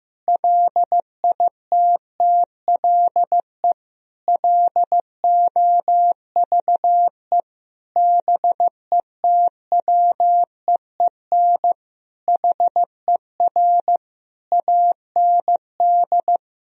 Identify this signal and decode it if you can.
Morse code, heard as a keyed tone, about 15 wpm, 700 Hz. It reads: LITTLE LOVE BETWEEN HER AND
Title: Sentences from Top 1000 Words - Review Part 1 - 15wpm